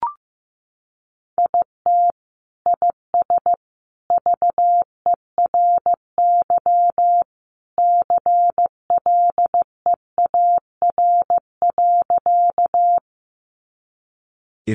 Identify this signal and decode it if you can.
Morse code, heard as a keyed tone, about 15 wpm, 700 Hz. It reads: IT IS VERY CLEAR.
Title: Sentences from Top 500 Words 15wpm